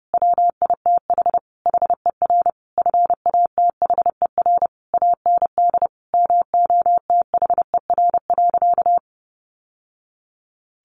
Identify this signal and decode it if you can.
Morse code, heard as a keyed tone, about 30 wpm, 700 Hz. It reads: WITH HER FATHER AND MOTHER.
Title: Sentences from Top 1100 Words 30wpm